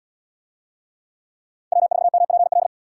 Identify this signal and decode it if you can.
Morse code, heard as a keyed tone, about 78 wpm, 700 Hz. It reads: W4MPL